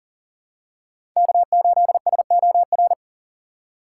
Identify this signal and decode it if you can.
Morse code, heard as a keyed tone, about 40 wpm, 700 Hz. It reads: K8SOR
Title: USA Call Signs - 5 Characters - 6x Character Spacing - 40wpm